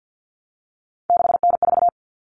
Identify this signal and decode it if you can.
Morse code, heard as a keyed tone, about 50 wpm, 700 Hz. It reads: 6N4
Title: Common Cut Numbers - 3-Digit 50wpm